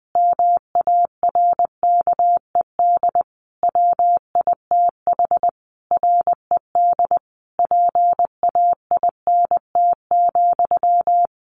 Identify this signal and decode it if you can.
Morse code, heard as a keyed tone, about 20 wpm, 700 Hz. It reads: MARKED WITH RED PAINT,
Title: A Sound Of Thunder 20wpm